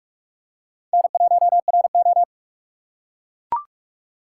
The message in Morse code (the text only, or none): N1RO